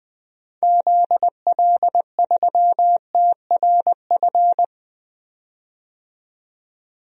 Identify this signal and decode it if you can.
Morse code, heard as a keyed tone, about 20 wpm, 700 Hz. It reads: ZL3TRF
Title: DXCC-100 Call Signs - 6 Characters - 20wpm